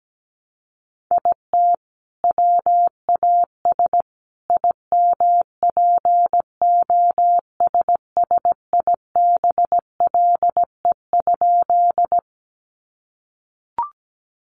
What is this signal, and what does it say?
Morse code, heard as a keyed tone, about 17 wpm, 700 Hz. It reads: IT WAS IMPOSSIBLE?
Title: Sentences from Top 1100 Words 17wpm